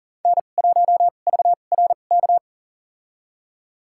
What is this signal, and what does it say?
Morse code, heard as a keyed tone, about 40 wpm, spaced as for 29 wpm, 700 Hz. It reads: N1VRK
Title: USA Call Signs - 5 Characters - 2x Character Spacing - 40wpm